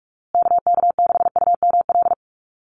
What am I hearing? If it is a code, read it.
Morse code, heard as a keyed tone, about 44 wpm, 700 Hz. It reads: KK6UML
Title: USA Call Signs Repeated 3x 45wpm